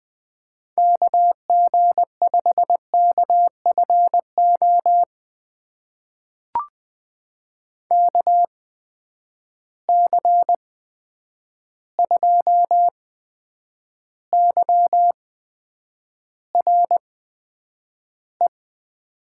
Morse code, heard as a keyed tone, about 20 wpm, spaced as for 9 wpm, 700 Hz. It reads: KG5KFO KC2YRE
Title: USA Call Signs - 6 Characters - 8x Character Spacing - 20wpm